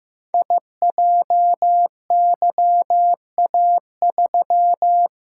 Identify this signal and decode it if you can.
Morse code, heard as a keyed tone, about 15 wpm, 700 Hz. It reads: IJYA3
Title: Code Groups - 5 Letter:Numbers - 15wpm